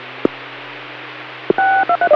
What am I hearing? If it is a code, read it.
DTMF, heard as telephone touch tones, tones of about 80 ms. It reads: 623